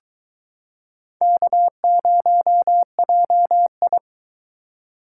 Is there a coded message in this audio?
Yes: Morse code from 1.2 to 4.0 s, heard as a keyed tone, about 22 wpm, 700 Hz. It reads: K0JI